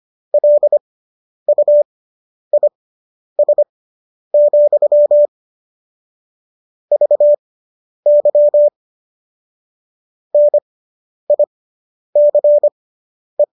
Morse code, heard as a keyed tone, about 25 wpm, spaced as for 10 wpm, 590 Hz. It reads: LUIS, VY NICE